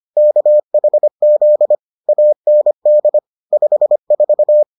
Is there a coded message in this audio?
Yes: Morse code from 0.2 to 4.6 s, heard as a keyed tone, about 25 wpm, 600 Hz. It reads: KHZ AND 54